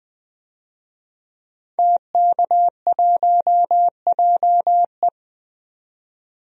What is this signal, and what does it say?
Morse code, heard as a keyed tone, about 20 wpm, 700 Hz. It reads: TK1JE
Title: DXCC-100 Call Signs - 5 Characters - 20wpm